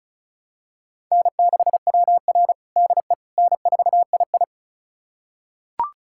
Morse code, heard as a keyed tone, about 36 wpm, 700 Hz. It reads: N6WR DE N4II
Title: USA Callsign DE Callsign - 4 Characters - 35wpm